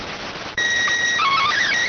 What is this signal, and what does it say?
SSTV (slow-scan television), PD 290.